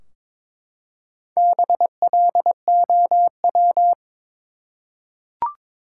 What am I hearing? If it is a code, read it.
Morse code, heard as a keyed tone, about 22 wpm, 700 Hz. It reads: BLOW
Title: Four Letter Words - 22wpm